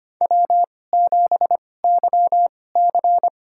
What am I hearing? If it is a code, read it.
Morse code, heard as a keyed tone, about 25 wpm, spaced as for 18 wpm, 700 Hz. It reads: W7YC
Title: USA Call Signs - 2x Character Spacing - 25wpm